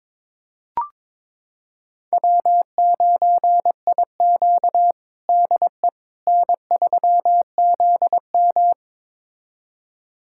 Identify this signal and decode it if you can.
Morse code, heard as a keyed tone, about 22 wpm, 700 Hz. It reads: W9IQ DE N3ZM
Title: USA Callsign DE Callsign - 4 Characters - 22wpm